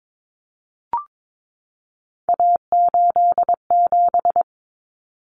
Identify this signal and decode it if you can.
Morse code, heard as a keyed tone, about 22 wpm, 700 Hz. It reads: A87
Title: Common Cut Numbers - 3-Digit 22wpm